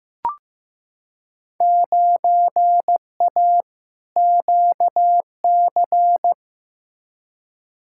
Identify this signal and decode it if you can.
Morse code, heard as a keyed tone, about 15 wpm, 700 Hz. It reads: 9A QC
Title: Field Day Exchanges Encore - 15wpm